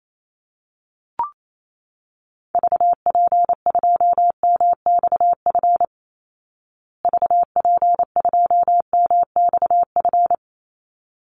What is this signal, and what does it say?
Morse code, heard as a keyed tone, about 28 wpm, 700 Hz. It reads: VP2MXF VP2MXF
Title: Call Signs Repeated 3x - 28wpm